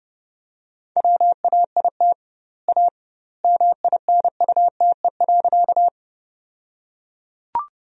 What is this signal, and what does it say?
Morse code, heard as a keyed tone, about 30 wpm, 700 Hz. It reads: WAIT A MINUTE.
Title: 2x Word-Spacing - Sentences from Top 500 Words - 30wpm